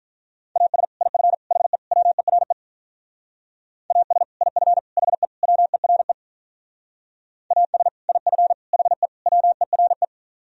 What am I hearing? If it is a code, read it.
Morse code, heard as a keyed tone, about 43 wpm, 700 Hz. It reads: AS IF HE WERE AS IF HE WERE AS IF HE WERE
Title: Speed-Racing - Sets of 4 Words 35wpm